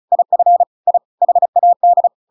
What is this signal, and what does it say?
Morse code, heard as a keyed tone, about 34 wpm, 700 Hz. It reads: IF I HAD